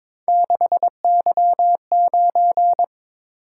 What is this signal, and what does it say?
Morse code, heard as a keyed tone, about 22 wpm, 700 Hz. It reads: 6Y9
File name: practice / 22wpm / Three Letter-Number Combinations - 22wpm